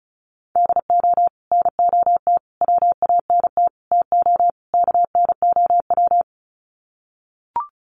Morse code, heard as a keyed tone, about 34 wpm, 700 Hz. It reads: DO NOT WANT TO KNOW
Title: Sets of 5 Words Encore 35wpm